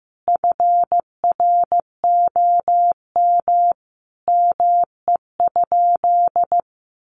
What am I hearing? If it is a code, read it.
Morse code, heard as a keyed tone, about 15 wpm, 700 Hz. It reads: FROM ME?
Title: Sentences from Top 1500 Words 15wpm